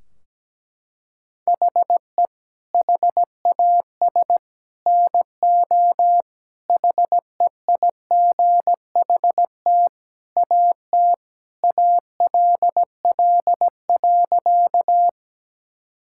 Morse code, heard as a keyed tone, about 17 wpm, 700 Hz. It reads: HE HAS NO HEIGHT AT ALL.
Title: Sentences from Top 1600 Words 17wpm